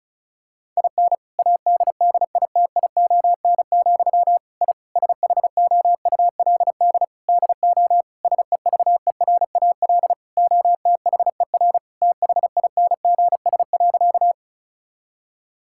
Morse code, heard as a keyed tone, about 36 wpm, 700 Hz. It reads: IN ADDITION, I SHOULD DO SEVERAL OTHER THINGS.
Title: Sentences from Top 1000 Words - Review Part 6 - 35wpm